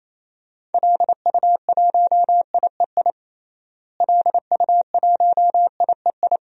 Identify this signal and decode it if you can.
Morse code, heard as a keyed tone, about 28 wpm, 700 Hz. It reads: LU1IEI LU1IEI
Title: DXCC-100 Call Signs - 6 Characters - Repeated 2x - 28wpm